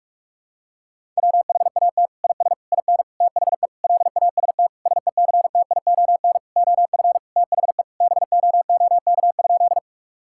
Morse code, heard as a keyed tone, about 41 wpm, 700 Hz. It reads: WHAT IS IN THE LAST SECTION OF THE BOOK?
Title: Sentences from Top 1000 Words - Review Part 5 - 45wpm